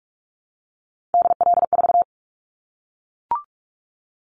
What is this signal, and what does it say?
Morse code, heard as a keyed tone, about 44 wpm, 700 Hz. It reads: DL4
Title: Callsign Prefixes for Top 50 DXCC - 45wpm